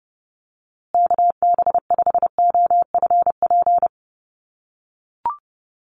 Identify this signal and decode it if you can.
Morse code, heard as a keyed tone, about 30 wpm, 700 Hz. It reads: KB5OFP
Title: USA Call Signs Repeated 2x 30wpm